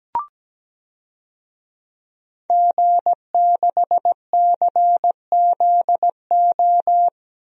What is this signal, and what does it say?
Morse code, heard as a keyed tone, about 17 wpm, 700 Hz. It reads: G6CZO
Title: DXCC-100 Call Signs - 5 Characters - 17wpm